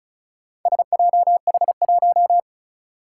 Morse code, heard as a keyed tone, about 36 wpm, 700 Hz. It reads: SJH1
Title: Four Challenging Letter-Number Combinations 35wpm